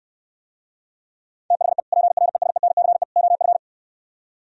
Morse code, heard as a keyed tone, about 66 wpm, 700 Hz. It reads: THE PRESENCE OF